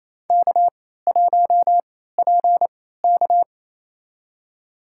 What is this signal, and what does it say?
Morse code, heard as a keyed tone, about 27 wpm, spaced as for 16 wpm, 700 Hz. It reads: K1PK
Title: USA Call Signs - 3x Character Spacing - 28wpm